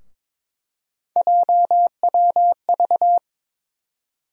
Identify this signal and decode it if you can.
Morse code, heard as a keyed tone, about 22 wpm, 700 Hz. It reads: JWV
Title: USA Call Sign Postfixes - 3 Letters 22wpm